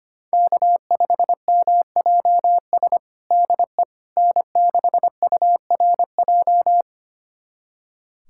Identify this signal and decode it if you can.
Morse code, heard as a keyed tone, about 25 wpm, 700 Hz. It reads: K5MJS DE N6URJ
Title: USA Callsign DE Callsign - 5 Characters - 25wpm